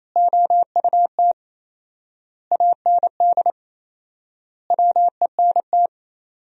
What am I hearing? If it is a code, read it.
Morse code, heard as a keyed tone, about 28 wpm, 700 Hz. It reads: OUT AND WENT
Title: 4x Word-Spacing - Sentences from Top 300 Words - 28wpm